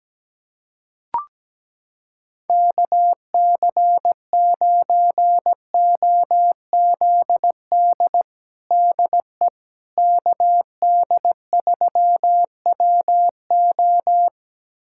Morse code, heard as a keyed tone, about 17 wpm, 700 Hz. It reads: KC9OZD DE KD3WO
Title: USA Callsign DE Callsign 17wpm